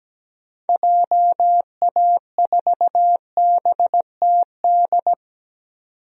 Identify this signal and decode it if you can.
Morse code, heard as a keyed tone, about 17 wpm, 700 Hz. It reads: JA4BTD
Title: DXCC-100 Call Signs - 6 Characters - 17wpm